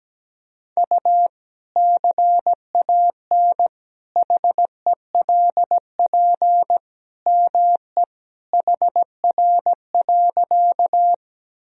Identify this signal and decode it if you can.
Morse code, heard as a keyed tone, about 17 wpm, 700 Hz. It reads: U CAN HELP ME HR.
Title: Sentences from Top 200 Words and Abbreviations 17wpm